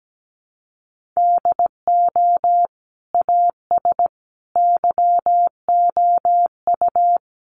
Morse code, heard as a keyed tone, about 17 wpm, 700 Hz. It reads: DO AS YOU
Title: Sentences from Top 1500 Words 17wpm